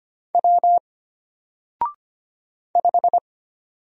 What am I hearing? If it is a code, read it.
Morse code, heard as a keyed tone, about 25 wpm, 700 Hz. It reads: W 5
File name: Single-Letter-Number-25wpm